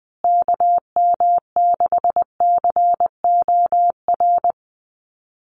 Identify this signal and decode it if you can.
Morse code, heard as a keyed tone, about 20 wpm, 700 Hz. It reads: KM6COR
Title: USA Call Signs Repeated 3x 20wpm